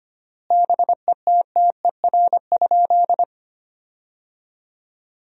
Morse code, heard as a keyed tone, about 25 wpm, 700 Hz. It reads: BETTER?
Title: 3x Word-Spacing - Sentences from Top 400 Words - 25wpm